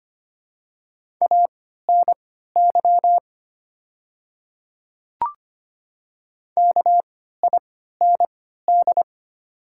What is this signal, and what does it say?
Morse code, heard as a keyed tone, about 25 wpm, spaced as for 14 wpm, 700 Hz. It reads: ANY KIND
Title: 3x Character Spacing - Top 200 Words - 25wpm